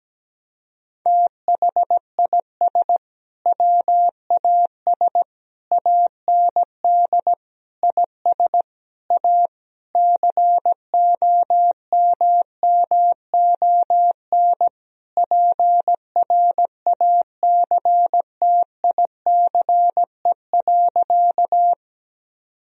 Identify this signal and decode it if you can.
Morse code, heard as a keyed tone, about 17 wpm, 700 Hz. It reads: THIS WAS AND IS A COMMON PRACTICE.
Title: Sentences from Top 1000 Words - Review Part 4 - 17wpm